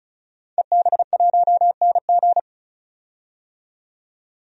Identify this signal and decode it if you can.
Morse code, heard as a keyed tone, about 36 wpm, 700 Hz. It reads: EB1NG
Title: DXCC-100 Call Signs - 5 Characters - 35wpm